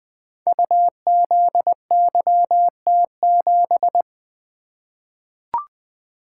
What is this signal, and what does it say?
Morse code, heard as a keyed tone, about 20 wpm, 700 Hz. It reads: UZYT7
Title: Code Groups - 5 Letter:Numbers - 20wpm